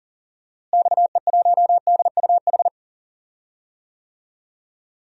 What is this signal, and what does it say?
Morse code, heard as a keyed tone, about 40 wpm, 700 Hz. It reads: XE1DUH
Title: DXCC-100 Call Signs - 6 Characters - Repeated 2x - 40wpm